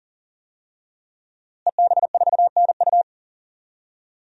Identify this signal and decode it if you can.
Morse code, heard as a keyed tone, about 37 wpm, 700 Hz. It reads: EB4NU
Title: DXCC-100 Call Signs - 5 Characters - Repeated 2x - 40wpm